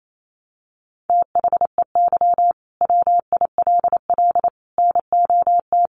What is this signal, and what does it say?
Morse code, heard as a keyed tone, about 28 wpm, 700 Hz. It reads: THEY WILL NOT
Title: Sentences from Top 1400 Words 28wpm